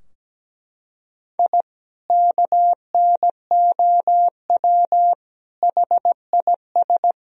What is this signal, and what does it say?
Morse code, heard as a keyed tone, about 17 wpm, 700 Hz. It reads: I KNOW HIS
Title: Sentences from Top 1000 Words - Review Part 3 - 17wpm